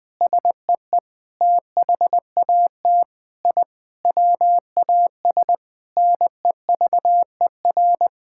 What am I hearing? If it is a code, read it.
Morse code, heard as a keyed tone, about 20 wpm, 700 Hz. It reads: SEE THAT I WAS NEVER
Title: Sentences from Top 1000 Words - Review Part 6 - 20wpm